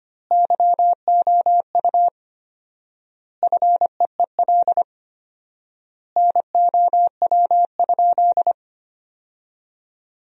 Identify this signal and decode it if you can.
Morse code, heard as a keyed tone, about 25 wpm, 700 Hz. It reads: YOU FEEL NOW?
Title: 4x Word-Spacing - Sentences from Top 400 Words - 25wpm